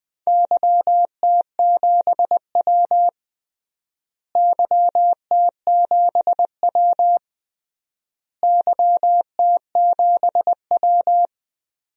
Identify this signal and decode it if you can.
Morse code, heard as a keyed tone, about 21 wpm, 700 Hz. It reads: YT7W YT7W YT7W
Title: DXCC-100 Call Signs - 4 Characters - Repeated 3x - 20wpm